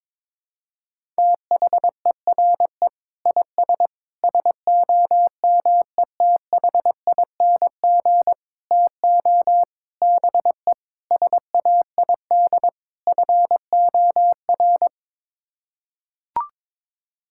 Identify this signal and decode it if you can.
Morse code, heard as a keyed tone, about 22 wpm, 700 Hz. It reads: THERE IS SOMETHING TO BE SAID FOR
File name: practice / 22wpm / Sets of 7 Words - 22wpm